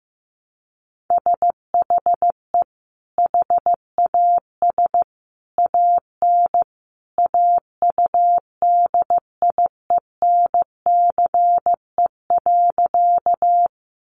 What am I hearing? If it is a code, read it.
Morse code, heard as a keyed tone, about 15 wpm, 700 Hz. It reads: SHE HAS AN AUDIENCE.
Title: Sentences from Top 800 Words 15wpm